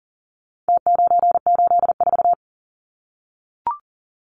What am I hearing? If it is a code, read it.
Morse code, heard as a keyed tone, about 40 wpm, 700 Hz. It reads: T984
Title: Numbers - Contest Style - 4-Digit 40wpm